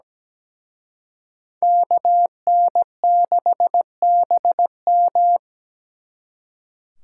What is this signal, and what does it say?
Morse code, heard as a keyed tone, about 20 wpm, 700 Hz. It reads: KN6BM
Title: Speed Racing - USA 2x2 Call Signs 17wpm